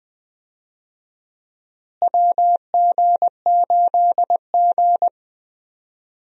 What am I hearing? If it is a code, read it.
Morse code, heard as a keyed tone, about 20 wpm, 700 Hz. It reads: WG8G
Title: USA Call Signs - 4 Characters - Repeated 2x - 20wpm